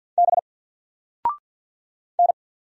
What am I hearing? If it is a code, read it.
Morse code, heard as a keyed tone, about 48 wpm, 700 Hz. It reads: B N
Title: Single Letter-Number Alt 50wpm